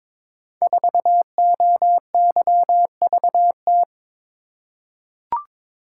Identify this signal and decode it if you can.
Morse code, heard as a keyed tone, about 21 wpm, 700 Hz. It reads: 4OYVT
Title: Code Groups - 5 Letter:Numbers - 22wpm